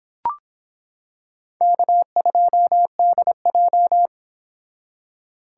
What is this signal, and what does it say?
Morse code, heard as a keyed tone, about 25 wpm, 700 Hz. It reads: K2DJ